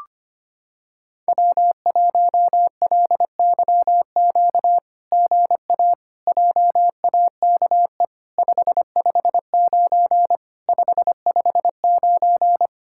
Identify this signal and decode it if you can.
Morse code, heard as a keyed tone, about 27 wpm, 700 Hz. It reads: W1LYQ GA JAKE 559 559
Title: SOTA QSO Practice 25wpm